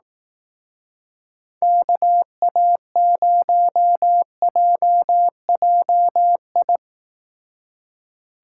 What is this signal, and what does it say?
Morse code, heard as a keyed tone, about 18 wpm, 700 Hz. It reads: KA0JJI
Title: Speed Racing - USA Call Signs 15wpm